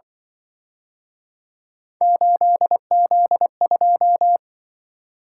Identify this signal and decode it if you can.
Morse code, heard as a keyed tone, about 24 wpm, 700 Hz. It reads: 8Z2
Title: Speed-Racing - Three Letter-Number Combinations - 20wpm